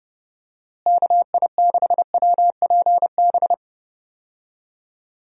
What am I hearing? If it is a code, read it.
Morse code, heard as a keyed tone, about 30 wpm, 700 Hz. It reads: KI6WPB
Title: USA Call Signs - 6 Characters - Repeated 3x - 30wpm